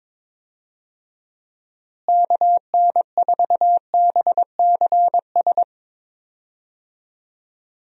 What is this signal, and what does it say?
Morse code, heard as a keyed tone, about 22 wpm, 700 Hz. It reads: KN4BCS
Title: USA Call Signs - 6 Characters - Repeated 2x - 22wpm